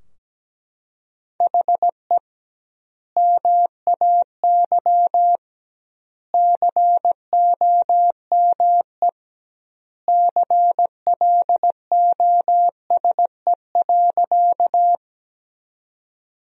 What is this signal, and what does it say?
Morse code, heard as a keyed tone, about 17 wpm, 700 Hz. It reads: HE MAY COME CLOSE.
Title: 2x Word-Spacing - Sentences from Top 300 Words - 17wpm